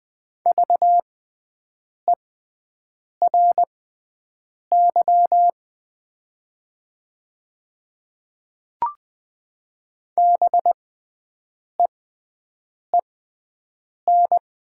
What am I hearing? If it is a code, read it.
Morse code, heard as a keyed tone, about 20 wpm, spaced as for 7 wpm, 700 Hz. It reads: VERY BEEN
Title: 6x Character Spacing - Top 200 Words - 20wpm